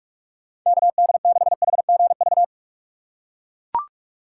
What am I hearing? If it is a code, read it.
Morse code, heard as a keyed tone, about 42 wpm, 700 Hz. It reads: KD6HGV
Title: USA Call Signs - 6 Characters - 8x Character Spacing - 45wpm